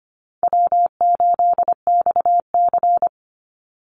Morse code, heard as a keyed tone, about 24 wpm, 700 Hz. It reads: W8XC
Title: Easy Call Signs - 25wpm